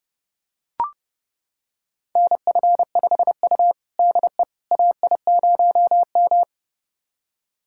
Morse code, heard as a keyed tone, about 30 wpm, 700 Hz. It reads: NF5U DE AI0M
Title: USA Callsign DE Callsign - 4 Characters - 30wpm